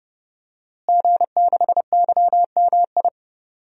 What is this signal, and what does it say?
Morse code, heard as a keyed tone, about 30 wpm, 700 Hz. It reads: G6YMI
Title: DXCC-100 Call Signs - 5 Characters - Repeated 2x - 30wpm